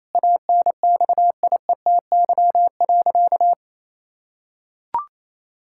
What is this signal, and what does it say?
Morse code, heard as a keyed tone, about 27 wpm, 700 Hz. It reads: ANXIETY.
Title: Sentences from Top 1600 Words 28wpm